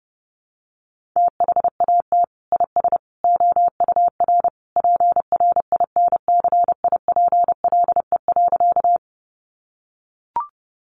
Morse code, heard as a keyed tone, about 30 wpm, 700 Hz. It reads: THAT IS OUR PRINCIPLE.
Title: Sentences from Top 1400 Words 30wpm